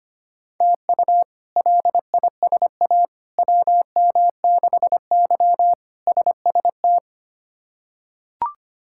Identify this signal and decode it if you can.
Morse code, heard as a keyed tone, about 25 wpm, 700 Hz. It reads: TU LISA WM6Y SST
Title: K1USN SST QSO Practice - 25wpm